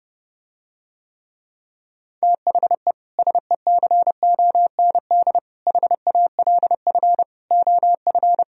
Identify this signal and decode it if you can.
Morse code, heard as a keyed tone, about 30 wpm, 700 Hz. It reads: THE SECOND HALF OF